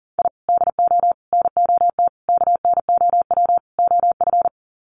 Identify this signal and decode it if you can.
Morse code, heard as a keyed tone, about 40 wpm, 700 Hz. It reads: I DO NOT KNOW OF
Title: Sets of 5 Words Encore2 40wpm